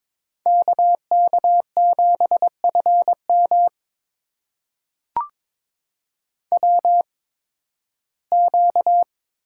Morse code, heard as a keyed tone, about 22 wpm, spaced as for 14 wpm, 700 Hz. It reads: KK7FM WQ
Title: USA Call Signs - 8x Character Spacing - 22wpm